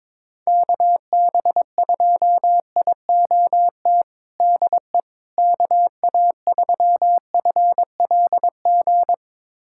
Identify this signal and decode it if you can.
Morse code, heard as a keyed tone, about 22 wpm, 700 Hz. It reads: KB2IOT DE KA3FLG
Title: USA Callsign DE Callsign - 6 Characters - 22wpm